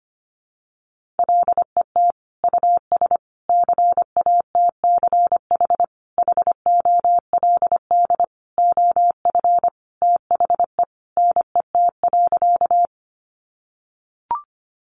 Morse code, heard as a keyed tone, about 25 wpm, 700 Hz. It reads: LET US CATCH HOLD OF THE NET.
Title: Sentences from Top 1500 Words 25wpm